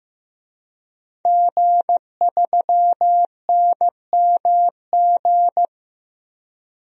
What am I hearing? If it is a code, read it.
Morse code, heard as a keyed tone, about 15 wpm, 700 Hz. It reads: G3NMG
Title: DXCC-100 Call Signs - 5 Characters - 15wpm